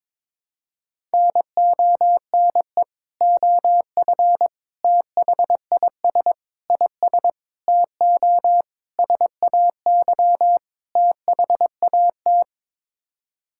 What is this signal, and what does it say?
Morse code, heard as a keyed tone, about 22 wpm, 700 Hz. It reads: NONE OF THIS IS TO SAY THAT
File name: practice / 22wpm / Sets of 7 Words - 22wpm